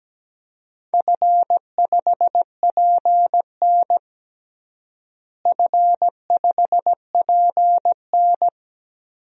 Morse code, heard as a keyed tone, about 17 wpm, 700 Hz. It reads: F5PN F5PN
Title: DXCC-100 Call Signs - 4 Characters - Repeated 2x - 17wpm